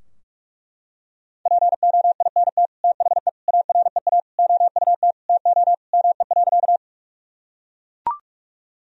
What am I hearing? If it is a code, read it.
Morse code, heard as a keyed tone, about 41 wpm, 700 Hz. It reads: POINT THE AREA OUT TO ME.
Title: Sentences from Top 1000 Words - Review Part 2 - 45wpm